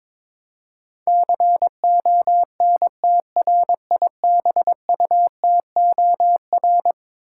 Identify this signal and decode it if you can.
Morse code, heard as a keyed tone, about 22 wpm, 700 Hz. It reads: CONTRIBUTOR
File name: practice / 22wpm / Eleven Letter Words - 22wpm